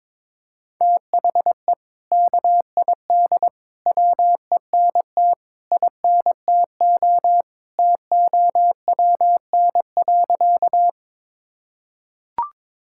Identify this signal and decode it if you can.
Morse code, heard as a keyed tone, about 22 wpm, 700 Hz. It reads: THE KID WENT INTO TOWN.
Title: Sentences from Top 1600 Words 22wpm